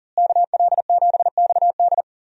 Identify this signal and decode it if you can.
Morse code, heard as a keyed tone, about 38 wpm, 700 Hz. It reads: KL7XD
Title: DXCC-100 Call Signs - 5 Characters - Repeated 3x - 40wpm